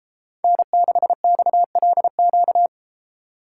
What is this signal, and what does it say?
Morse code, heard as a keyed tone, about 33 wpm, 700 Hz. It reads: N6XLQ